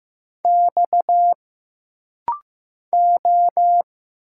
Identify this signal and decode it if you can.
Morse code, heard as a keyed tone, about 15 wpm, 700 Hz. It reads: X O
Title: Single Letters - Rapid-Fire 15wpm